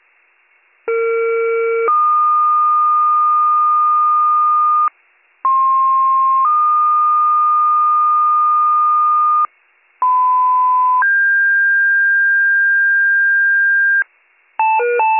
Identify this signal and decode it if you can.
Two-tone page, heard as a one-second tone, then a three-second tone, about 460 then 1220 Hz, about 1060 then 1220 Hz, about 1010 then 1600 Hz. It is heard from 0.9 to 14.0 s.